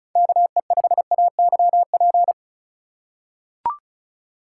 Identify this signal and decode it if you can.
Morse code, heard as a keyed tone, about 36 wpm, 700 Hz. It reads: KE5AYP
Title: USA Call Signs - 6 Characters - 3x Character Spacing - 35wpm